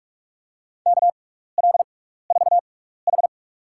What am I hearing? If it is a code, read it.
Morse code, heard as a keyed tone, about 47 wpm, spaced as for 16 wpm, 700 Hz. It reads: KL4H
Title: USA Call Signs - 4 Characters - 6x Character Spacing - 45wpm